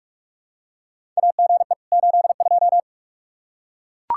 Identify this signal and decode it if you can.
Morse code, heard as a keyed tone, about 42 wpm, 700 Hz. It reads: AGE 82
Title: K1USN SST QSO Practice 45wpm